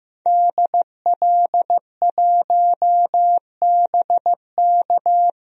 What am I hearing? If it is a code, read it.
Morse code, heard as a keyed tone, about 15 wpm, 700 Hz. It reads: DL1BK
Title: DXCC-100 Call Signs - 5 Characters - 15wpm